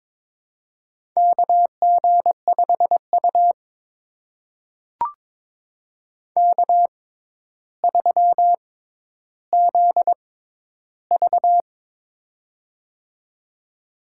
Morse code, heard as a keyed tone, about 22 wpm, spaced as for 11 wpm, 700 Hz. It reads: KG5U K3ZV
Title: USA Call Signs - 6x Character Spacing - 22wpm